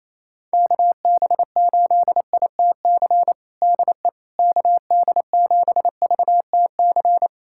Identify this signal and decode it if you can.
Morse code, heard as a keyed tone, about 28 wpm, 700 Hz. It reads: KB8ITC DE KD7VTC
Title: USA Callsign DE Callsign - 6 Characters - 28wpm